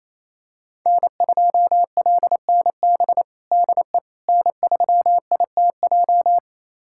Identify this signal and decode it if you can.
Morse code, heard as a keyed tone, about 28 wpm, 700 Hz. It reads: N2LNB DE N3ITJ
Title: USA Callsign DE Callsign - 5 Characters - 28wpm